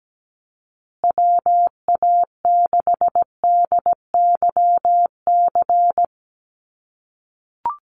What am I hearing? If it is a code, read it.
Morse code, heard as a keyed tone, about 17 wpm, 700 Hz. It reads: WA6DYC
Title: USA Call Signs Repeated 3x 17wpm